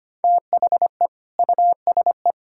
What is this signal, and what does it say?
Morse code, heard as a keyed tone, about 25 wpm, 700 Hz. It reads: THE USE
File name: practice / 25wpm / Sentences with 2-Word Phrases - 25wpm